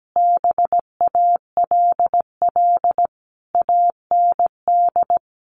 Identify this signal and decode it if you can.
Morse code, heard as a keyed tone, about 17 wpm, 700 Hz. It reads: BALL AND
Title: Sentences from Top 900 Words 17wpm